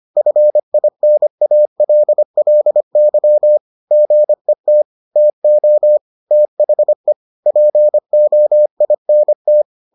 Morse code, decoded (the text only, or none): FINALLY GET TO THE POINT